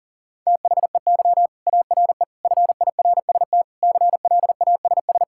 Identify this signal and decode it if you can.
Morse code, heard as a keyed tone, about 40 wpm, 700 Hz. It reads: THEY ARE FIRST CLASS
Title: Sentences from Top 1000 Words - Review Part 5 - 40wpm